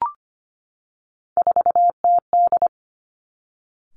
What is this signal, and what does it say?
Morse code, heard as a keyed tone, about 25 wpm, 700 Hz. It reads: 4TD
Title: Three Challenging Letter-Number Combinations 25wpm